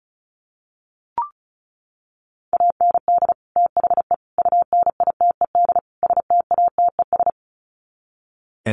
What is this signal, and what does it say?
Morse code, heard as a keyed tone, about 34 wpm, 700 Hz. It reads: AND THE UNITED STATES
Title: Sets of 4 Words 35wpm